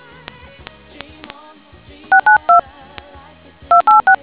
DTMF, heard as telephone touch tones, tones of about 120 ms. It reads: 692 286